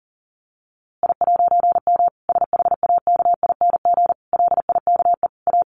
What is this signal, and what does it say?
Morse code, heard as a keyed tone, about 40 wpm, 700 Hz. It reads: I'M SHAKING LIKE A
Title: A Sound Of Thunder 40wpm